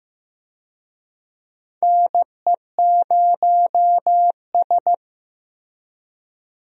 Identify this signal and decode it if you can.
Morse code, heard as a keyed tone, about 16 wpm, 700 Hz. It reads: NE0S
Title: USA Call Signs - 4 Characters - Repeated 3x - 15wpm